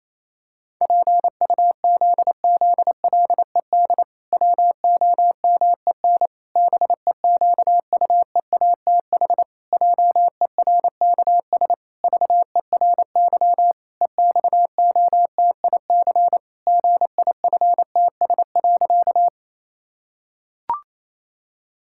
Morse code, heard as a keyed tone, about 28 wpm, 700 Hz. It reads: PUZZLED WOMEN BEQUEATH JERKS VERY EXOTIC GIFTS.
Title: Pangram Challenge v2 - 28wpm